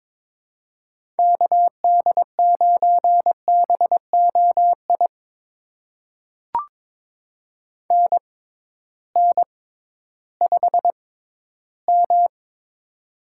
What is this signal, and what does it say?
Morse code, heard as a keyed tone, about 22 wpm, spaced as for 13 wpm, 700 Hz. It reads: KD9BOI NN5M
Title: USA Call Signs - 6x Character Spacing - 22wpm